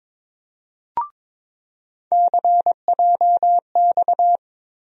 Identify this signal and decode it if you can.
Morse code, heard as a keyed tone, about 22 wpm, 700 Hz. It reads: CJX
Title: USA Call Sign Postfixes - 3 Letters 22wpm